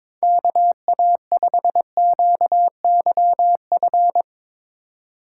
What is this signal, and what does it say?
Morse code, heard as a keyed tone, about 21 wpm, 700 Hz. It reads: KA5QYF
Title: USA Call Signs - 6 Characters - 2x Character Spacing - 22wpm